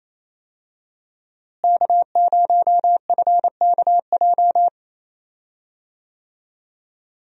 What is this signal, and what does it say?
Morse code, heard as a keyed tone, about 28 wpm, 700 Hz. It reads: K0FKJ
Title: USA Call Signs - 5 Characters - Repeated 2x - 28wpm